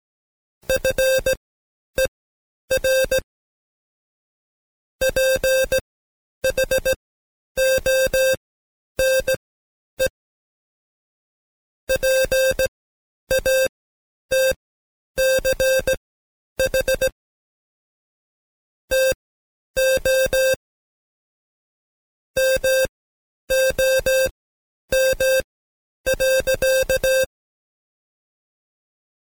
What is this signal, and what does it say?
Morse code, heard as a keyed tone, about 18 wpm, spaced as for 10 wpm, 530 Hz. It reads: FER PHONE PATCH TO MOM.
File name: Code Quick / AudioDownld2 / Seg131-Stepper4 9WPM